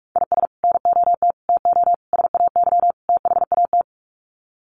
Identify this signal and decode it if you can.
Morse code, heard as a keyed tone, about 44 wpm, 700 Hz. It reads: IS NOT TO SAY THAT
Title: Sets of 5 Words 45wpm